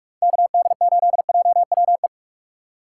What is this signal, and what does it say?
Morse code, heard as a keyed tone, about 41 wpm, 700 Hz. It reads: KD8JWE
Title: USA Call Signs - 3x Character Spacing - 45wpm